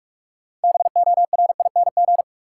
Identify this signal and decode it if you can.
Morse code, heard as a keyed tone, about 46 wpm, 700 Hz. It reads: BORING
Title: Six Letter Words - 45wpm